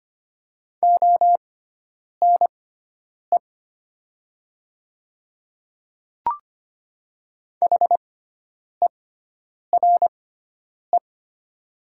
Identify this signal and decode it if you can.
Morse code, heard as a keyed tone, about 25 wpm, spaced as for 9 wpm, 700 Hz. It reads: ONE HERE
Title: 6x Character Spacing - Top 300 Words - 25wpm